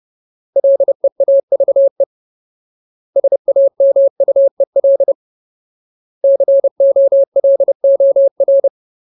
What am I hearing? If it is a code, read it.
Morse code, heard as a keyed tone, about 30 wpm, 550 Hz. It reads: LEAVE SAMUEL COLOR
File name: Ultimate Word List / 30WPM Ultimate